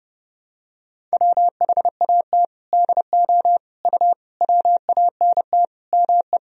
Morse code, heard as a keyed tone, about 30 wpm, 700 Hz. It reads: WHAT DO U WANT ME
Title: Sentences from Top 200 Words and Abbreviations 30wpm